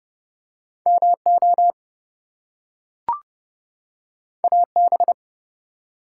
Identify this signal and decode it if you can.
Morse code, heard as a keyed tone, about 30 wpm, 700 Hz. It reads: MO AB
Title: US States and Canadian Provinces 30wpm